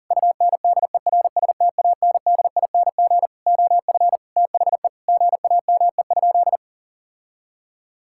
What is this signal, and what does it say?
Morse code, heard as a keyed tone, about 37 wpm, 700 Hz. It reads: UNDERSTANDING OF THE GAME?
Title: Sentences from Top 1000 Words - Review Part 5 - 40wpm